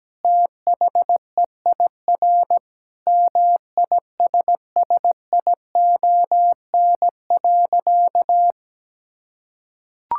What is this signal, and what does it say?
Morse code, heard as a keyed tone, about 17 wpm, 700 Hz. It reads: THEIR MISSION.
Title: Sentences from Top 1600 Words 17wpm